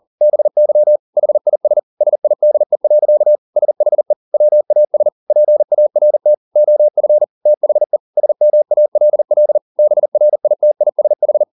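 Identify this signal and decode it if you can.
Morse code, heard as a keyed tone, about 40 wpm, 600 Hz. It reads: BY HIS SIDE. SHE WAS PART OF THE SMALL BRITISH
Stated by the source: Morse code at 40 wpm, practice text, CW med Gnister.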